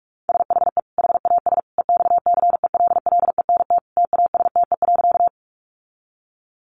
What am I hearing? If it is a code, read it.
Morse code, heard as a keyed tone, about 44 wpm, 700 Hz. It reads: SHE HAS EXCELLENT TASTE.
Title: Sentences from Top 1100 Words 45wpm